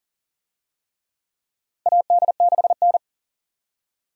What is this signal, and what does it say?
Morse code, heard as a keyed tone, about 40 wpm, 700 Hz. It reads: AD6N
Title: USA Call Signs - 4 Characters - 40wpm